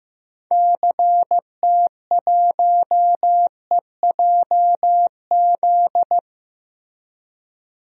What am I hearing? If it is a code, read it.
Morse code, heard as a keyed tone, about 15 wpm, 700 Hz. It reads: CT1EJZ
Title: DXCC-100 Call Signs - 6 Characters - Repeated 2x - 15wpm